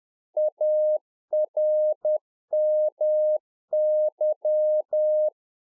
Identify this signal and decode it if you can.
Morse code, heard as a keyed tone, about 10 wpm, 610 Hz. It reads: ARMY
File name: Czech Army Morse Keys 10WPM